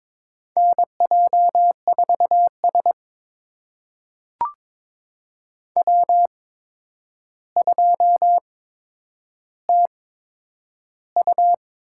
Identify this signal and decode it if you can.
Morse code, heard as a keyed tone, about 22 wpm, spaced as for 9 wpm, 700 Hz. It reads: NJ4S W2TU
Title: USA Call Signs - 8x Character Spacing - 22wpm